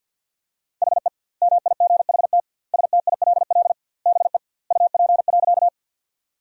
Morse code, heard as a keyed tone, about 50 wpm, 700 Hz. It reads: HE MIGHT STILL BE UP.
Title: 2x Word-Spacing - Sentences from Top 300 Words - 50wpm